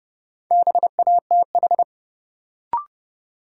Morse code, heard as a keyed tone, about 32 wpm, 700 Hz. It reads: BATH
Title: Sentences from Top 1600 Words - New Word Drill - 30wpm